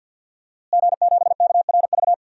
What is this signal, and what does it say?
Morse code, heard as a keyed tone, about 52 wpm, 700 Hz. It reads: G7KRV
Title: DXCC-100 Call Signs - 5 Characters - Repeated 2x - 50wpm